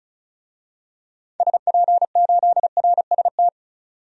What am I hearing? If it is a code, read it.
Morse code, heard as a keyed tone, about 34 wpm, 700 Hz. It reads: SP8RST
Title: DXCC-100 Call Signs - 6 Characters - 35wpm